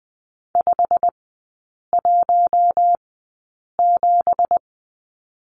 Morse code, heard as a keyed tone, about 20 wpm, spaced as for 8 wpm, 700 Hz. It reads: 517
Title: 191 Lesson 31e - Letter v - Three Characters 20wpm